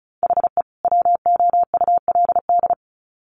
Morse code, heard as a keyed tone, about 34 wpm, 700 Hz. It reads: HE WOULD
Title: Sets of 2 Words 35wpm